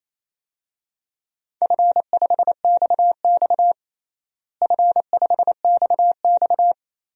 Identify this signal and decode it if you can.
Morse code, heard as a keyed tone, about 28 wpm, 700 Hz. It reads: F5XX F5XX
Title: DXCC-100 Call Signs - 4 Characters - Repeated 3x - 28wpm